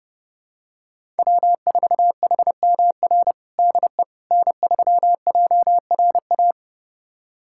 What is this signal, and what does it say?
Morse code, heard as a keyed tone, about 30 wpm, 700 Hz. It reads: W4HMR DE N3JRA
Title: USA Callsign DE Callsign - 5 Characters - 30wpm